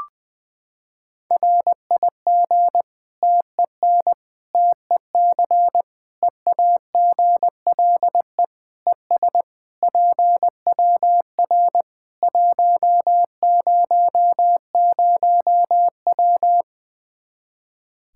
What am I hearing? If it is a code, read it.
Morse code, heard as a keyed tone, about 21 wpm, 700 Hz. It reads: RIG TEN TEC EAGLE ES PWR 100W
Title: Practice QSO - Exchange 2 - Info - 20wpm